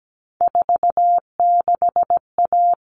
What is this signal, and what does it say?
Morse code, heard as a keyed tone, about 17 wpm, 700 Hz. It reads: 46A
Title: All Cut Numbers - 3-Digit 17wpm